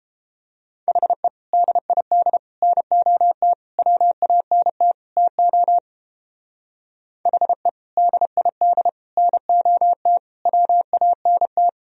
Morse code, heard as a keyed tone, about 31 wpm, 700 Hz. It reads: HE DID NOT WANT TO HE DID NOT WANT